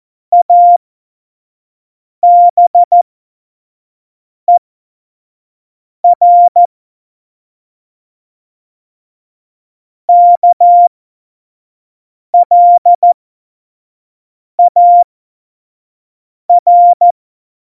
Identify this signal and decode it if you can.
Morse code, heard as a keyed tone, about 14 wpm, spaced as for 5 wpm, 700 Hz. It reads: ABER KLAR
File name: Morsekurs / Witze / Witz 30. Januar 2018, 25 F70